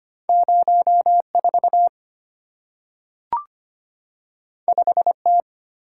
Morse code, heard as a keyed tone, about 25 wpm, 700 Hz. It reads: 04 5T
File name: Two Letter-Number Combinations - 25wpm